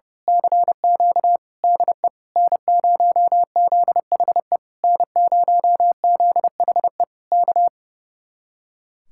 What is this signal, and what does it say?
Morse code, heard as a keyed tone, about 28 wpm, 700 Hz. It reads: CQ DE N0ZHE N0ZHE K
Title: Speed-Racing Common QSO Phrases - 30wpm